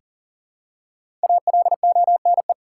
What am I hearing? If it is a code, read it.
Morse code, heard as a keyed tone, about 40 wpm, 700 Hz. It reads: ALONE
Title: Five Letter Words - 40wpm